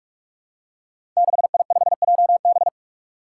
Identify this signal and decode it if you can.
Morse code, heard as a keyed tone, about 46 wpm, 700 Hz. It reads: 6I5JB
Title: Five Challenging Letter-Number Combinations 45wpm